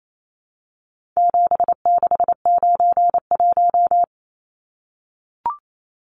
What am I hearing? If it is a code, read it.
Morse code, heard as a keyed tone, about 28 wpm, 700 Hz. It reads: 7691
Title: Numbers - 4-Digit 28wpm